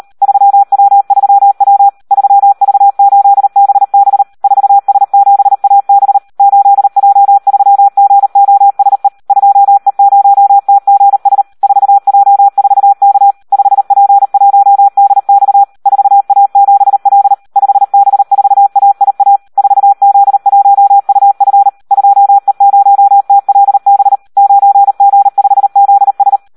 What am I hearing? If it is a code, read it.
Morse code, heard as a keyed tone, about 38 wpm, 800 Hz. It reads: 3W3W 3V8BB 4S7AB 8J3GOSE 2E0TGS 4J4K 5P1DX 4A7L 5B4AIA 4Z1UF 2E0TLB 9G5ZS
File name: Audio practica CW / SD 1 38